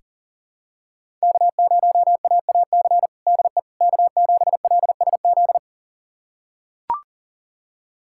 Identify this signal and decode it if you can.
Morse code, heard as a keyed tone, about 40 wpm, 690 Hz. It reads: K0AAC DE K7LSZ
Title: USA Callsign DE Callsign - 5 Characters - 40wpm